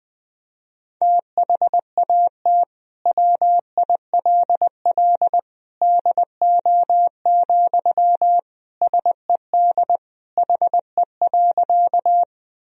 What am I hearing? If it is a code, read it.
Morse code, heard as a keyed tone, about 21 wpm, 700 Hz. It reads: THAT WILL DO, SED HE.
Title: Sentences from Top 100 Words and Abbreviations 20wpm